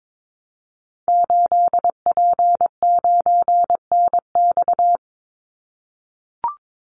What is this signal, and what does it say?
Morse code, heard as a keyed tone, about 22 wpm, 700 Hz. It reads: 8P9NX
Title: Call Signs - 22wpm